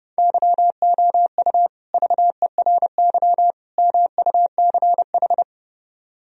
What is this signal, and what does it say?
Morse code, heard as a keyed tone, about 32 wpm, 700 Hz. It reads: YOU VERY MUCH